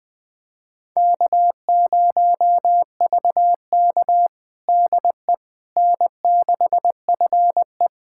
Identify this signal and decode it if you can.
Morse code, heard as a keyed tone, about 20 wpm, 700 Hz. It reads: K0VK DE N6FE
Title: USA Callsign DE Callsign - 4 Characters - 20wpm